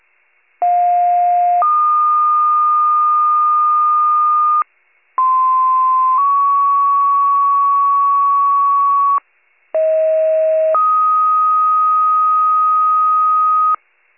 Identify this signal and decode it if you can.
Two-tone page, heard as a one-second tone, then a three-second tone, about 710 then 1220 Hz, about 1040 then 1120 Hz, about 640 then 1250 Hz. It is heard from 0.6 to 13.8 s.
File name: 84 FIRE